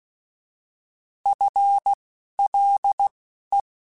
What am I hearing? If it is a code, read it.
Morse code, heard as a keyed tone, about 16 wpm, spaced as for 11 wpm, 800 Hz. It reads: FLE